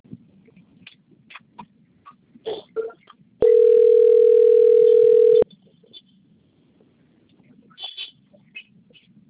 Ringback tone (North America), 440 + 480 Hz.
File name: out- - - - - .70391